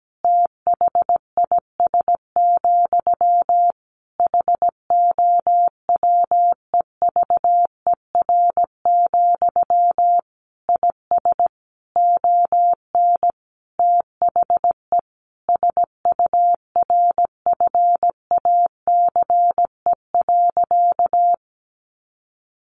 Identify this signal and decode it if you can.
Morse code, heard as a keyed tone, about 17 wpm, 700 Hz. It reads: THIS, HOWEVER, IS ON THE SURFACE.
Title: Sentences from Top 600 Words 17wpm